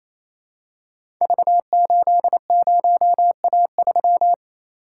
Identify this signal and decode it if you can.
Morse code, heard as a keyed tone, about 28 wpm, 700 Hz. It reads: V80A3
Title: Code Groups - 5 Letter:Numbers - 28wpm